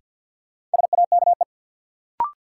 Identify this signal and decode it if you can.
Morse code, heard as a keyed tone, about 49 wpm, 700 Hz. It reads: SAKE
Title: Sets of 4 Words - New Word Drill - 50wpm